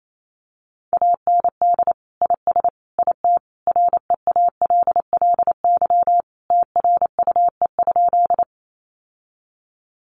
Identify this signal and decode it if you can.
Morse code, heard as a keyed tone, about 28 wpm, 700 Hz. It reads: AND IS IT REALLY TRUE?
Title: Sentences from Top 600 Words 28wpm